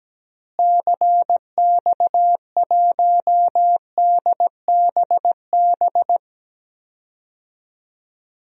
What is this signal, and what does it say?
Morse code, heard as a keyed tone, about 17 wpm, 700 Hz. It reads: CX1DBB
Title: DXCC-100 Call Signs - 6 Characters - Repeated 2x - 17wpm